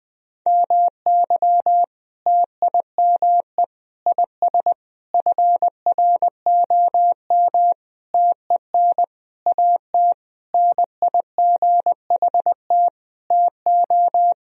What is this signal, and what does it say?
Morse code, heard as a keyed tone, about 20 wpm, 700 Hz. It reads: MY TIME IS FROM TEN AT NIGHT TO
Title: Sentences from Top 1000 Words - Review Part 2 - 20wpm